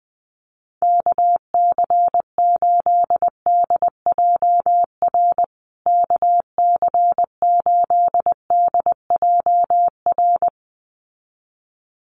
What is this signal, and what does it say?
Morse code, heard as a keyed tone, about 20 wpm, 700 Hz. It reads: KC8DJR KC8DJR
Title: USA Call Signs Repeated 2x 20wpm